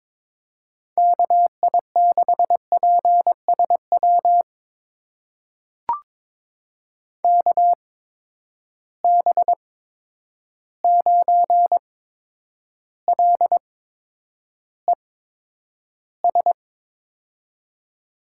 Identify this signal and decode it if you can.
Morse code, heard as a keyed tone, about 22 wpm, spaced as for 9 wpm, 700 Hz. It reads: KI6PSW KB9LES
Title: USA Call Signs - 6 Characters - 8x Character Spacing - 22wpm